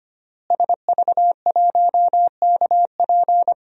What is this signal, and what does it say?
Morse code, heard as a keyed tone, about 25 wpm, 700 Hz. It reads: SV1KP
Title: DXCC-100 Call Signs - 5 Characters - Repeated 2x - 25wpm